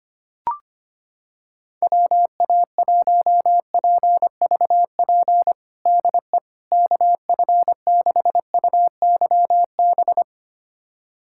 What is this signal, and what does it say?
Morse code, heard as a keyed tone, about 25 wpm, 700 Hz. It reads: WA1PVP DE KF6UYB
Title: USA Callsign DE Callsign - 6 Characters - 25wpm